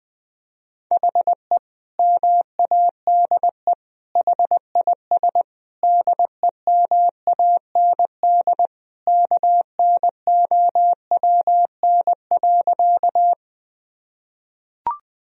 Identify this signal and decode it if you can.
Morse code, heard as a keyed tone, about 20 wpm, 700 Hz. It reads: HE MADE HIS DEMAND KNOWN.
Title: Sentences from Top 1000 Words - Review Part 4 - 20wpm